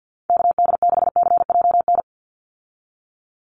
Morse code, heard as a keyed tone, about 50 wpm, 700 Hz. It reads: KD6CJN
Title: USA Call Signs 50wpm